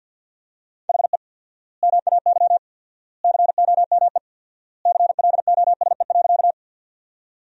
Morse code, heard as a keyed tone, about 51 wpm, 700 Hz. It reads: HE MAY COME CLOSE.
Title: 4x Word-Spacing - Sentences from Top 300 Words - 50wpm